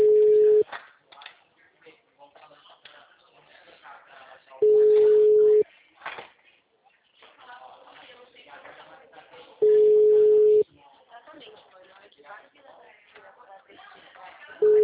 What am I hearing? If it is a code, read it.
Ringback tone (Europe, 425 Hz), with speech.